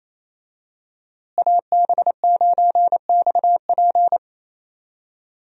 Morse code, heard as a keyed tone, about 28 wpm, 700 Hz. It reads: AB9XP
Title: USA Call Signs - 5 Characters - Repeated 2x - 28wpm